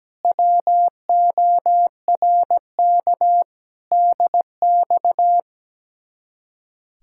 Morse code, heard as a keyed tone, about 17 wpm, 700 Hz. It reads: WORK DX
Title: Speed-Racing Common QSO Phrases Encore - 17wpm